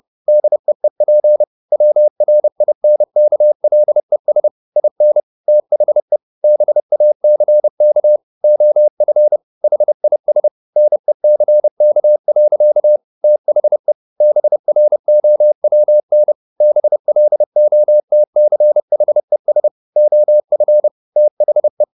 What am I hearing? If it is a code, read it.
Morse code, heard as a keyed tone, about 30 wpm, 600 Hz. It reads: DEEP WRINKLES IN THE BACK OF HIS NECK. THE BROWN BLOTCHES OF THE
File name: the old man and the sea / the old man and the sea 1 30wpm